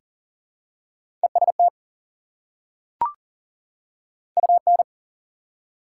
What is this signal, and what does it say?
Morse code, heard as a keyed tone, about 40 wpm, 700 Hz. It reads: EST UN
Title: Prefixes and Suffixes 40wpm